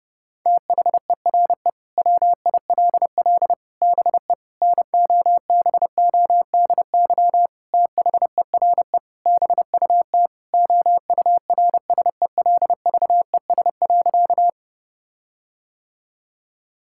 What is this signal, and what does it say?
Morse code, heard as a keyed tone, about 32 wpm, 700 Hz. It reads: THERE WILL BE NOBODY THERE BUT OURSELVES.
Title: Sentences from Top 1600 Words 30wpm